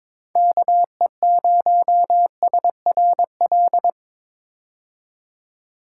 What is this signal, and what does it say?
Morse code, heard as a keyed tone, about 22 wpm, 700 Hz. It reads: KE0SRL
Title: K1USN SST QSO Practice - 22wpm